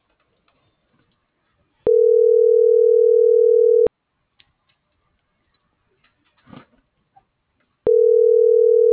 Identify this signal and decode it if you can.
Ringback tone (North America), 440 + 480 Hz.